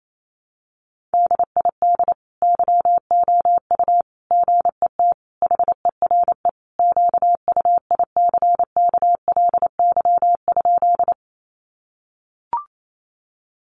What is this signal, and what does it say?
Morse code, heard as a keyed tone, about 28 wpm, 700 Hz. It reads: DID YOU GET HERE QUICKLY?
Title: Sentences from Top 700 Words 28wpm